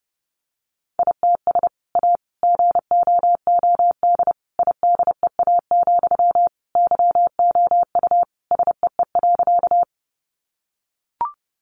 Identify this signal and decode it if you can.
Morse code, heard as a keyed tone, about 30 wpm, 700 Hz. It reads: ITS A GOOD IDEA, YOU SEE.
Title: Sentences from Top 600 Words 30wpm